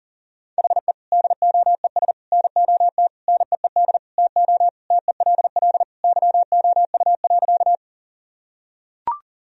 Morse code, heard as a keyed tone, about 40 wpm, 700 Hz. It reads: HE DOES NOT NEED TO TELL YOU.
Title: Sentences from Top 1000 Words - Review Part 1 - 40wpm